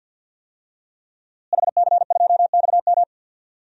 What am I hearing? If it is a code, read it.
Morse code, heard as a keyed tone, about 51 wpm, 700 Hz. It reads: HC2XM
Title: DXCC-100 Call Signs - 5 Characters - Repeated 2x - 50wpm